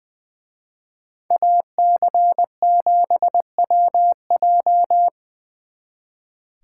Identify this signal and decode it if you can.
Morse code, heard as a keyed tone, about 23 wpm, 700 Hz. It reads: AC7WJ
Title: Speed Racing - USA 2x2 Call Signs 20wpm